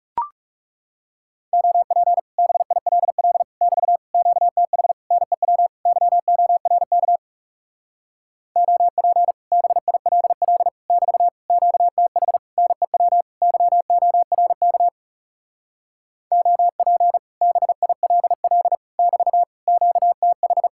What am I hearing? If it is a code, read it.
Morse code, heard as a keyed tone, about 40 wpm, 700 Hz. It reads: OP BILL <BT> QTH NEW YORK OP BILL <BT> QTH NEW YORK OP BILL <BT> QTH